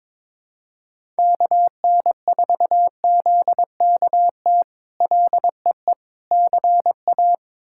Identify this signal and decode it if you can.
Morse code, heard as a keyed tone, about 22 wpm, 700 Hz. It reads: KN4ZKT LEE CA
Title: K1USN SST QSO Practice - 22wpm